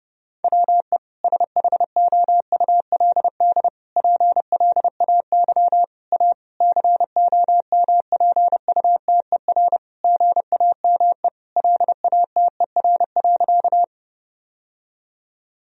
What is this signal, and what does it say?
Morse code, heard as a keyed tone, about 30 wpm, 700 Hz. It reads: WE SHOULD PLAY A COMPUTER GAME LATER.
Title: Sentences from Top 1000 Words - Review Part 3 - 30wpm